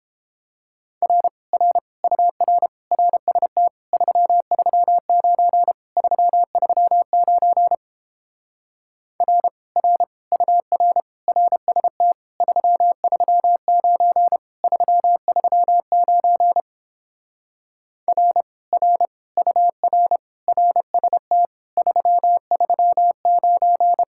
Morse code, heard as a keyed tone, about 31 wpm, 700 Hz. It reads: R R UR RST 339 339 R R UR RST 339 339 R R UR RST 339